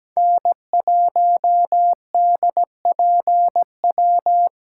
Morse code, heard as a keyed tone, about 17 wpm, 700 Hz. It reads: N1DPW
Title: USA Call Signs - 5 Characters - 4x Character Spacing - 17wpm